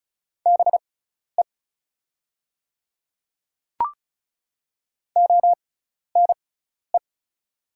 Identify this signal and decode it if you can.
Morse code, heard as a keyed tone, about 36 wpm, spaced as for 12 wpm, 700 Hz. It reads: BE ONE
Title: 6x Character Spacing - Top 100 Words - 35wpm